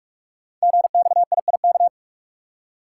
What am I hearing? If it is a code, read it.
Morse code, heard as a keyed tone, about 44 wpm, 690 Hz. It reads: GXIIK
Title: Code Groups - 5 Letters - 45wpm